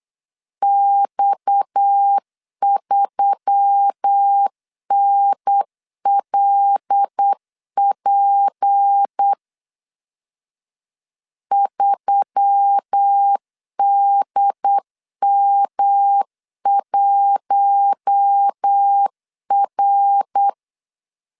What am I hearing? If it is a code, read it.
Morse code, heard as a keyed tone, about 8 wpm, 800 Hz. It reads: X3NLP 3DM1R
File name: 8wpm